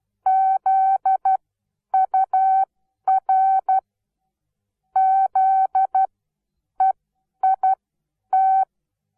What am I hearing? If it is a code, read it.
Morse code, heard as a keyed tone, about 12 wpm, spaced as for 9 wpm, 760 Hz. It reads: ZUR ZEIT
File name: Onlinelehrgang / morsen / DJ4UF Morsen17